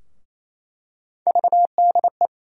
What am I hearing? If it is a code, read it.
Morse code, heard as a keyed tone, about 28 wpm, 700 Hz. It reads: VDE
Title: USA Call Sign Postfixes - 3 Letters 28wpm